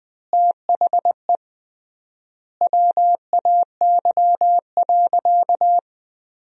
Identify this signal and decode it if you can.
Morse code, heard as a keyed tone, about 20 wpm, 700 Hz. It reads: THE WAY.
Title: 3x Word-Spacing - Sentences from Top 400 Words - 20wpm